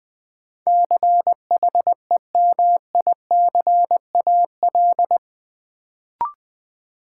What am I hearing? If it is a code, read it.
Morse code, heard as a keyed tone, about 21 wpm, 700 Hz. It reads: CHEMICAL
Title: Sentences from Top 1200 Words - New Word Drill - 20wpm